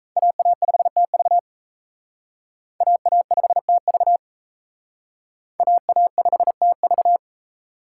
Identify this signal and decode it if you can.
Morse code, heard as a keyed tone, about 35 wpm, 700 Hz. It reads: AA5TV AA5TV AA5TV
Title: Speed Racing - USA 2x2 Call Signs 28wpm